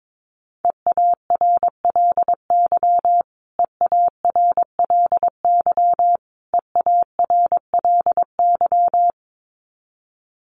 Morse code, heard as a keyed tone, about 22 wpm, 700 Hz. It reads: EARLY EARLY EARLY
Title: Top 500 Words Repeated 22wpm